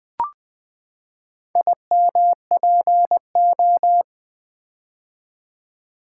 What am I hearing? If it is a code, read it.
Morse code, heard as a keyed tone, about 20 wpm, 700 Hz. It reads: IMPO
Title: Build A Word - 4 Letters At A Time - 10-Letter Words - 20wpm